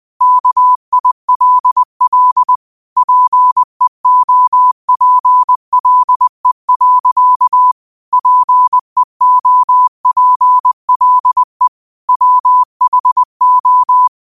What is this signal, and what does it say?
Morse code, heard as a keyed tone, about 20 wpm, 1000 Hz. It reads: KILL PEOPLE. PEOPLE WHO